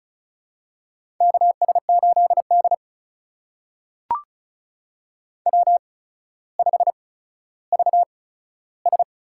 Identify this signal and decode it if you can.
Morse code, heard as a keyed tone, about 35 wpm, spaced as for 15 wpm, 700 Hz. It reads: KS8D W5VS
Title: USA Call Signs - 4 Characters - 8x Character Spacing - 35wpm